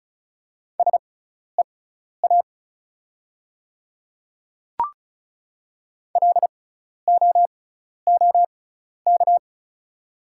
Morse code, heard as a keyed tone, about 36 wpm, spaced as for 12 wpm, 690 Hz. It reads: SEA LOOK
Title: 6x Character Spacing - Top 300 Words - 35wpm